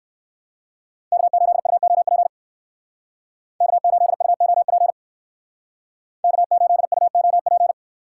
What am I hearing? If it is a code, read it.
Morse code, heard as a keyed tone, about 59 wpm, 700 Hz. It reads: K8UOP K8UOP K8UOP